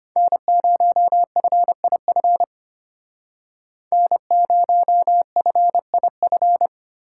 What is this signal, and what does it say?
Morse code, heard as a keyed tone, about 33 wpm, 700 Hz. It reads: N0FIF N0FIF